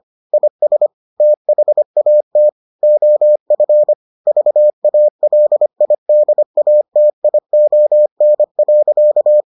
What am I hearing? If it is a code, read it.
Morse code, heard as a keyed tone, about 25 wpm, 600 Hz. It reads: IS THAT OF VALIDATION.